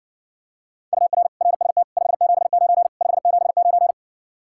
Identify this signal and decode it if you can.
Morse code, heard as a keyed tone, about 60 wpm, 700 Hz. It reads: UR RST 579 579